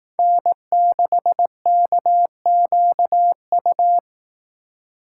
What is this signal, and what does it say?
Morse code, heard as a keyed tone, about 18 wpm, 700 Hz. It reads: N6KQU
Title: Speed Racing - USA Call Signs 15wpm